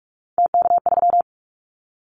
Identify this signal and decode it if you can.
Morse code, heard as a keyed tone, about 46 wpm, 700 Hz. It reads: TK3
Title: Callsign Prefixes for Top 100 DXCC - 45wpm